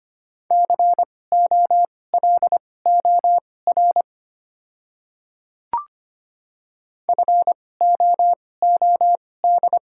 Morse code, heard as a keyed tone, about 25 wpm, spaced as for 18 wpm, 700 Hz. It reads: COLOR FOOD
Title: 2x Character Spacing - Top 500 Words - 25wpm